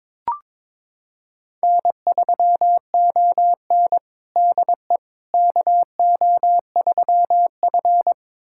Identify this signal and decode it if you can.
Morse code, heard as a keyed tone, about 22 wpm, 700 Hz. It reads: N3ON DE KO3F
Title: USA Callsign DE Callsign - 4 Characters - 22wpm